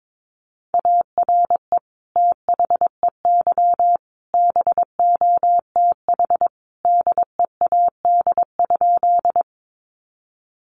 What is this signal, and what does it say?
Morse code, heard as a keyed tone, about 22 wpm, 700 Hz. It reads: ARE THEY BOTH DEAD?
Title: Sentences from Top 1500 Words 22wpm